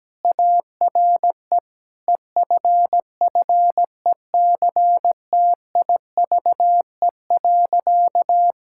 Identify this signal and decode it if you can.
Morse code, heard as a keyed tone, about 17 wpm, 700 Hz. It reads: ARE EFFECTIVE.
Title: Sentences from Top 1000 Words - Review Part 6 - 17wpm